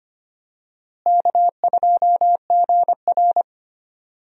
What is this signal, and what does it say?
Morse code, heard as a keyed tone, about 25 wpm, 700 Hz. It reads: K2GR
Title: USA Call Signs - 4 Characters - Repeated 2x - 25wpm